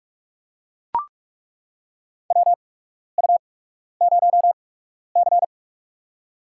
Morse code, heard as a keyed tone, about 43 wpm, spaced as for 12 wpm, 700 Hz. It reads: WU0C
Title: USA Call Signs - 8x Character Spacing - 45wpm